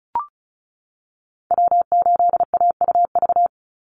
Morse code, heard as a keyed tone, about 34 wpm, 700 Hz. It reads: W8AUV
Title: USA Call Signs 35wpm